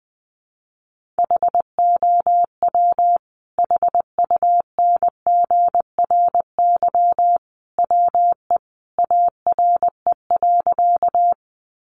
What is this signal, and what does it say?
Morse code, heard as a keyed tone, about 20 wpm, 700 Hz. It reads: HOW HUNGRY WE ARE.
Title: Sentences from Top 1400 Words 20wpm